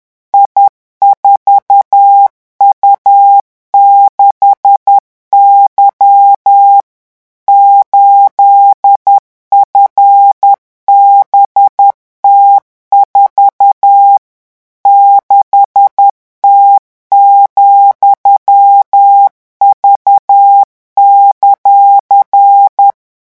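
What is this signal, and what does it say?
Morse code, heard as a keyed tone, about 10 wpm, 800 Hz. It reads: I4U6Y 8FBT4 6T,V;
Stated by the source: French Morse code revision lesson 18 at 10 words per minute.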